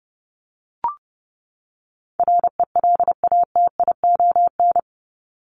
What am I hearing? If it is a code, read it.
Morse code, heard as a keyed tone, about 30 wpm, 700 Hz. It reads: RELATION
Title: ES:S Suffix - Top 100 Words with Base Words 30wpm